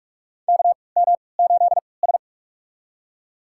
Morse code, heard as a keyed tone, about 44 wpm, spaced as for 25 wpm, 690 Hz. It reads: KM8S
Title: USA Call Signs - 4 Characters - 3x Character Spacing - 45wpm